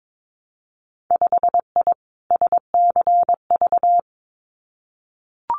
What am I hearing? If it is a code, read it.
Morse code, heard as a keyed tone, about 22 wpm, 700 Hz. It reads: 5I SCV
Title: Winter Field Day Exchanges Encore 22wpm